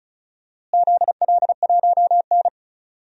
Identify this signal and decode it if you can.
Morse code, heard as a keyed tone, about 36 wpm, 690 Hz. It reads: ZL1N
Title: DXCC-100 Call Signs - 4 Characters - Repeated 3x - 35wpm